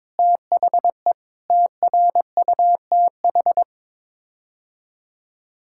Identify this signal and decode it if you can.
Morse code, heard as a keyed tone, about 22 wpm, 700 Hz. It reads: THE TRUTH
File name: practice / 22wpm / Sets of 8 Words - 22wpm